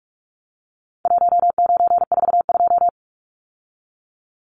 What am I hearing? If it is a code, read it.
Morse code, heard as a keyed tone, about 44 wpm, 700 Hz. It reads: 1942
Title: Numbers - Contest Style - 4-Digit 45wpm